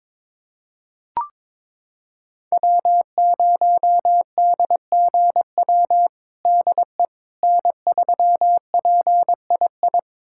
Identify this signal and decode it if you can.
Morse code, heard as a keyed tone, about 22 wpm, 700 Hz. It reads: W0DGW DE N3PII
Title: USA Callsign DE Callsign - 5 Characters - 22wpm